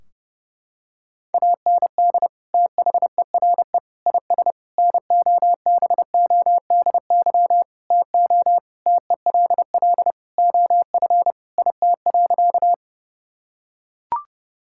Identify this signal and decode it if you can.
Morse code, heard as a keyed tone, about 32 wpm, 700 Hz. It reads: AND THERE IS NOBODY TO TELL OF IT.
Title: Sentences from Top 1600 Words 30wpm